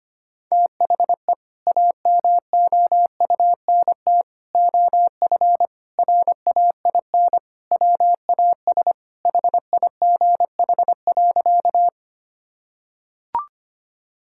Morse code, heard as a keyed tone, about 25 wpm, 700 Hz. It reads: THE AMOUNT OF RAIN WAS HIGH.
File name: practice / 25wpm / Sentences with 3-Word Phrases - 25wpm